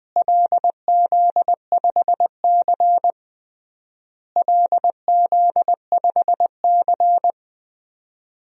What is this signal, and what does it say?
Morse code, heard as a keyed tone, about 20 wpm, 700 Hz. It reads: LZ5C LZ5C
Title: DXCC-100 Call Signs - 4 Characters - Repeated 3x - 20wpm